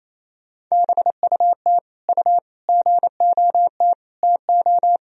Morse code, heard as a keyed tone, about 28 wpm, 700 Hz. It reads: BUT U GOT TO
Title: Sentences from Top 300 Words and Abbreviations 28wpm